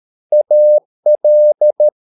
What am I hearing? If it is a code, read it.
Morse code, heard as a keyed tone, about 13 wpm, 600 Hz. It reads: AL